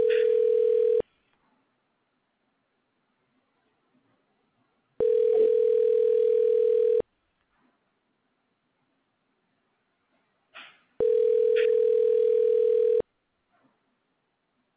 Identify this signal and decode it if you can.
Ringback tone (North America), 440 + 480 Hz.